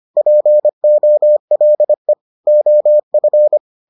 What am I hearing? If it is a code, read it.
Morse code, heard as a keyed tone, about 25 wpm, 600 Hz. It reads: POLE OF